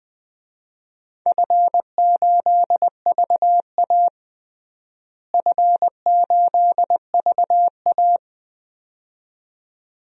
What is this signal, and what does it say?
Morse code, heard as a keyed tone, about 20 wpm, 700 Hz. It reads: F8VA F8VA
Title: DXCC-100 Call Signs - 4 Characters - Repeated 2x - 20wpm